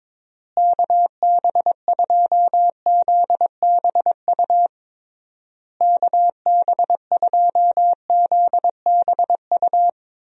Morse code, heard as a keyed tone, about 22 wpm, 700 Hz. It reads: KB2ZBU KB2ZBU
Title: USA Call Signs - 6 Characters - Repeated 3x - 22wpm